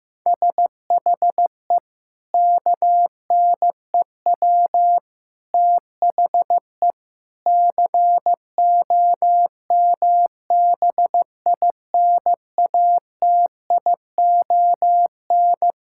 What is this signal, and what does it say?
Morse code, heard as a keyed tone, about 16 wpm, 700 Hz. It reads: SHE KNEW THE COMBINATION
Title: Sentences from Top 1000 Words - Review Part 6 - 15wpm